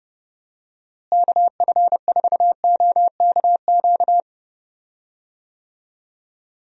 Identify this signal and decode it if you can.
Morse code, heard as a keyed tone, about 30 wpm, 700 Hz. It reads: KF4OKQ
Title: USA Call Signs - 6 Characters - Repeated 3x - 30wpm